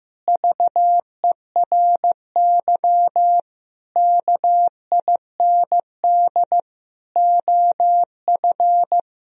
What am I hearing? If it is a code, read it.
Morse code, heard as a keyed tone, about 15 wpm, 700 Hz. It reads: VERY KIND OF
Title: Sentences from Top 1000 Words - Review Part 1 - 15wpm